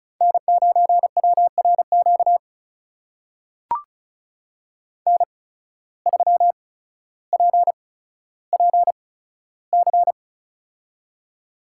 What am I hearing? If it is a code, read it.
Morse code, heard as a keyed tone, about 36 wpm, spaced as for 15 wpm, 700 Hz. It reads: N9WRQ N3PPC
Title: USA Call Signs - 5 Characters - 8x Character Spacing - 35wpm